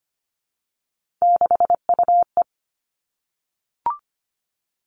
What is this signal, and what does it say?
Morse code, heard as a keyed tone, about 25 wpm, 700 Hz. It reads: 6UE
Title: Three Challenging Letter-Number Combinations 25wpm